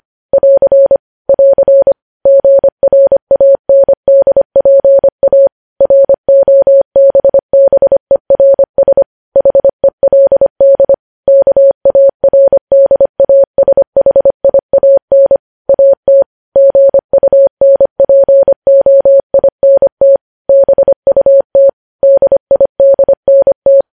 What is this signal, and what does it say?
Morse code, heard as a keyed tone, about 25 wpm, 550 Hz. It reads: <AR> <AR> GRANDPA ROBBERS HELD KARDASHIAN AT GUNPOINT BUT DIDNT
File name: podcast / News headlines-1x-25x25